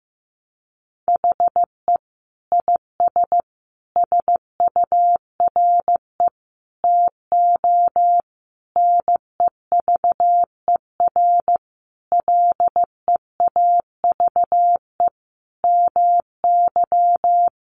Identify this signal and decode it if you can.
Morse code, heard as a keyed tone, about 14 wpm, 700 Hz. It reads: HE IS SURE TO NEVER LEAVE MY
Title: Sentences from Top 700 Words 15wpm